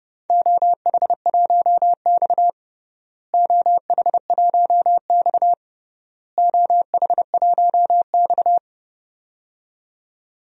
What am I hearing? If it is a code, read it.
Morse code, heard as a keyed tone, about 32 wpm, 700 Hz. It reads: OH1X OH1X OH1X
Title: DXCC-100 Call Signs - 4 Characters - Repeated 3x - 30wpm